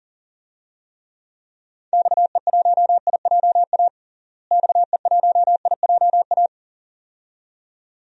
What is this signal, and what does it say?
Morse code, heard as a keyed tone, about 37 wpm, 700 Hz. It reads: XE1IJA XE1IJA
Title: DXCC-100 Call Signs - 6 Characters - Repeated 2x - 40wpm